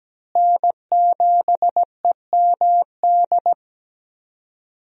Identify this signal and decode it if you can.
Morse code, heard as a keyed tone, about 17 wpm, 700 Hz. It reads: N7EMD
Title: USA Call Signs - 5 Characters - Repeated 3x - 17wpm